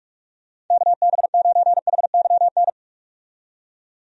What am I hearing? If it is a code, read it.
Morse code, heard as a keyed tone, about 42 wpm, 700 Hz. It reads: KB9HYN
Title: USA Call Signs - 6 Characters - 45wpm